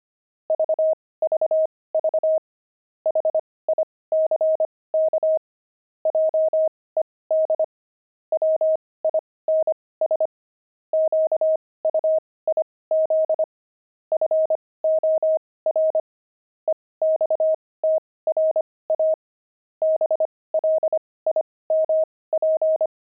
Morse code, heard as a keyed tone, about 25 wpm, spaced as for 18 wpm, 630 Hz. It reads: VVV HICK JED WINS QUIZ FOR EXTRA BLIMP
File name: pangram-fs-18-cs-25-blimp